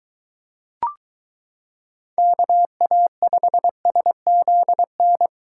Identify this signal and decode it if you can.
Morse code, heard as a keyed tone, about 23 wpm, 700 Hz. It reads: KA5SZN